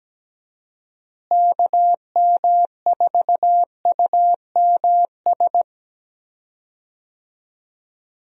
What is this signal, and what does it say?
Morse code, heard as a keyed tone, about 17 wpm, 700 Hz. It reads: KM4UMS
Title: USA Call Signs - 6 Characters - Repeated 2x - 17wpm